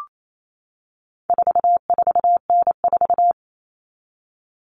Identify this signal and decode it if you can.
Morse code, heard as a keyed tone, about 28 wpm, 700 Hz. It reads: 44N4
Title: All Cut Numbers - 4-Digit 28wpm